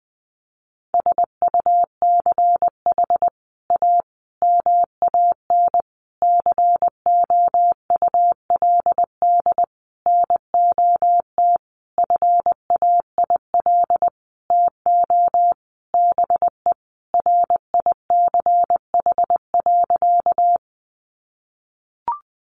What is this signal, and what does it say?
Morse code, heard as a keyed tone, about 20 wpm, 700 Hz. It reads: SUCH A MAN COULD NOT FAIL TO BE RICH.
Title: Sentences from Top 1100 Words 20wpm